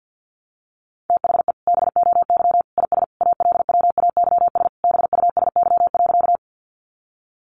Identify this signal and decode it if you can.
Morse code, heard as a keyed tone, about 50 wpm, 700 Hz. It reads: THE BOY IS ALWAYS BUSY.
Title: Additional Easy Phrases 50wpm